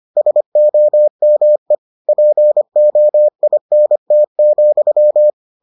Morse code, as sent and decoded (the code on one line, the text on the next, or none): ... --- -- . / .--. --- .. -. - --..--
SOME POINT,